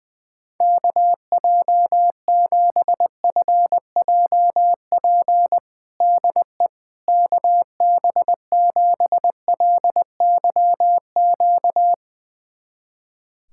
Morse code, heard as a keyed tone, about 20 wpm, 700 Hz. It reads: KJ7FJP DE KB7LYQ
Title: USA Callsign DE Callsign - 6 Characters - 20wpm